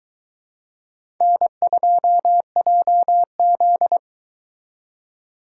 Morse code, heard as a keyed tone, about 22 wpm, 700 Hz. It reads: N2JZ